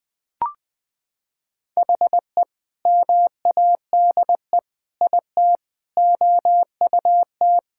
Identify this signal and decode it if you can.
Morse code, heard as a keyed tone, about 21 wpm, 700 Hz. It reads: HE MADE IT OUT
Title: Sentences with 3-Word Phrases - 20wpm